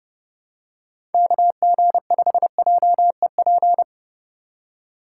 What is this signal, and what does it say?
Morse code, heard as a keyed tone, about 29 wpm, 700 Hz. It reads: KG5JEP